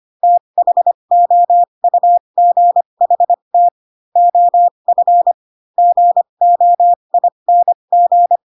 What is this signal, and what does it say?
Morse code, heard as a keyed tone, about 25 wpm, spaced as for 22 wpm, 700 Hz. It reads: THOUGHT OF GOING